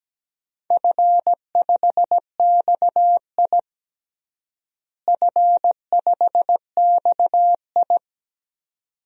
Morse code, heard as a keyed tone, about 17 wpm, 700 Hz. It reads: F5XI F5XI
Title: DXCC-100 Call Signs - 4 Characters - Repeated 2x - 17wpm